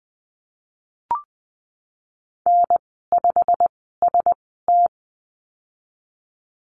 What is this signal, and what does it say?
Morse code, heard as a keyed tone, about 20 wpm, spaced as for 15 wpm, 700 Hz. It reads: N5ST
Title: 128 Lesson 21g - Number 6 - Call Signs 20wpm